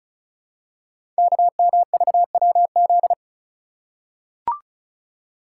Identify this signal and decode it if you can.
Morse code, heard as a keyed tone, about 36 wpm, 700 Hz. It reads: KMVWZ
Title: Code Groups - 5 Letters - 35wpm